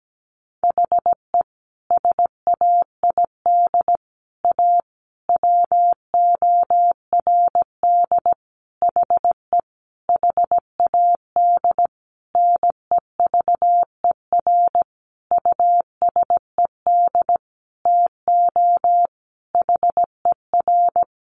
Morse code, heard as a keyed tone, about 17 wpm, 700 Hz. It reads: HE SAID A WORD HE HAD NEVER USED TO HER
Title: Sentences from Top 600 Words 17wpm